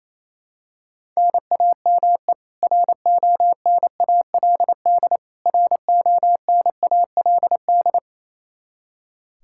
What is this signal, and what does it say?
Morse code, heard as a keyed tone, about 28 wpm, 700 Hz. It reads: NAME RONALD RONALD
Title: Practice QSO - Exchange 1 - Beginning - 28wpm